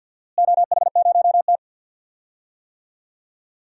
Morse code, heard as a keyed tone, about 47 wpm, 700 Hz. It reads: OH0T
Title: DXCC-100 Call Signs - 4 Characters - Repeated 2x - 50wpm